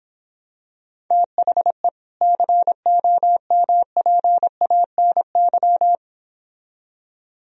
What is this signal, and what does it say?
Morse code, heard as a keyed tone, about 26 wpm, 700 Hz. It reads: THE COMPANY